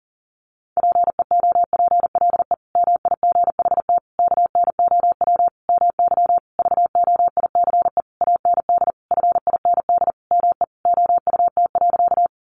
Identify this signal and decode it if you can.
Morse code, heard as a keyed tone, about 40 wpm, 700 Hz. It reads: PEOPLE MIGHT KNOW MY VOICE AND FIND ME OUT.
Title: Sentences from Top 500 Words 40wpm